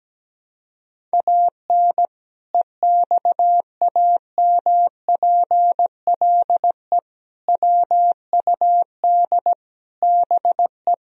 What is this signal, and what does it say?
Morse code, heard as a keyed tone, about 17 wpm, 700 Hz. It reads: AN EXAMPLE WUD BE
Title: Sentences from Top 400 Words and Abbreviations 17wpm